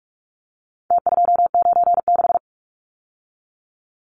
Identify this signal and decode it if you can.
Morse code, heard as a keyed tone, about 44 wpm, 700 Hz. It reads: T296
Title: Numbers - Contest Style - 4-Digit 45wpm